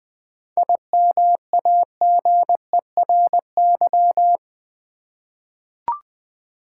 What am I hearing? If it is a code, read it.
Morse code, heard as a keyed tone, about 20 wpm, 700 Hz. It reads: IMAGERY
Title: 7 Letter Words - First 3 Letters Given - 20wpm